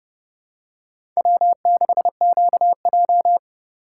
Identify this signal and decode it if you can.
Morse code, heard as a keyed tone, about 31 wpm, 700 Hz. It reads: W6QJ
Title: USA Call Signs - 4 Characters - 4x Character Spacing - 30wpm